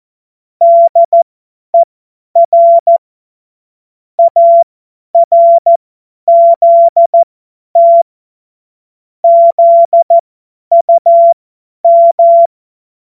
Morse code, heard as a keyed tone, about 14 wpm, spaced as for 10 wpm, 680 Hz. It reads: DER ARZT ZUM
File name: Morsekurs / Witze / Witz 19 März 2024 50 F70